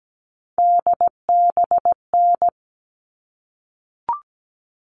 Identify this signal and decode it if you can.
Morse code, heard as a keyed tone, about 17 wpm, 700 Hz. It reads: DBN
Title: All Cut Numbers - 3-Digit 17wpm